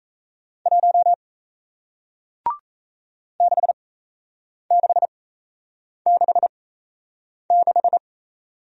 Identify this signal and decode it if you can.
Morse code, heard as a keyed tone, about 37 wpm, spaced as for 8 wpm, 700 Hz. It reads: 1 6666